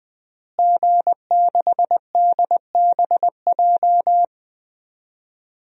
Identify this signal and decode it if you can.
Morse code, heard as a keyed tone, about 20 wpm, 700 Hz. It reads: G6DBJ
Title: DXCC-100 Call Signs - 5 Characters - Repeated 3x - 20wpm